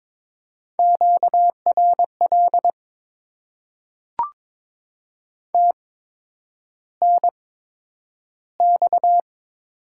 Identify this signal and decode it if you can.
Morse code, heard as a keyed tone, about 22 wpm, spaced as for 9 wpm, 700 Hz. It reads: QRL TNX
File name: Top 20 QSO Elements - 8x Character Spacing - 22wpm